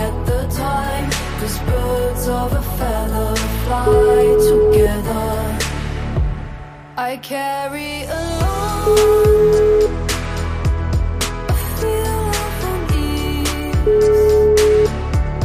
Ringback tone (Europe, 425 Hz), with noise.